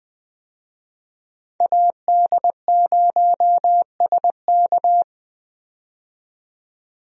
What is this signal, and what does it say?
Morse code, heard as a keyed tone, about 20 wpm, 700 Hz. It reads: AD0SK
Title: USA Call Signs - 5 Characters - 20wpm